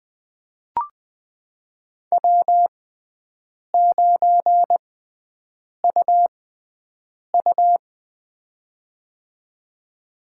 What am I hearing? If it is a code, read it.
Morse code, heard as a keyed tone, about 20 wpm, spaced as for 7 wpm, 700 Hz. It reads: W9UU
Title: USA Call Signs - 4 Characters - 6x Character Spacing - 20wpm